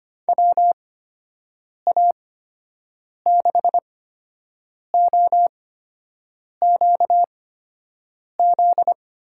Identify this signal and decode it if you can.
Morse code, heard as a keyed tone, about 25 wpm, spaced as for 7 wpm, 700 Hz. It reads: WA6OQZ
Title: USA Call Signs - 6 Characters - 8x Character Spacing - 25wpm